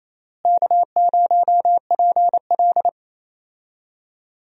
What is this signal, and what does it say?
Morse code, heard as a keyed tone, about 28 wpm, 700 Hz. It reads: K0PL
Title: USA Call Signs - 4 Characters - Repeated 3x - 28wpm